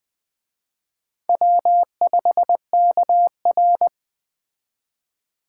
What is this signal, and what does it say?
Morse code, heard as a keyed tone, about 21 wpm, 700 Hz. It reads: W5KR
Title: Speed Racing - USA Call Signs 17wpm